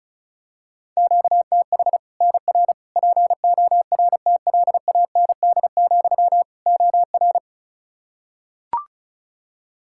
Morse code, heard as a keyed tone, about 34 wpm, 700 Hz. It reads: QTH NR PORTLAND, OR
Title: Practice QSO - Exchange 1 - Beginning - 35wpm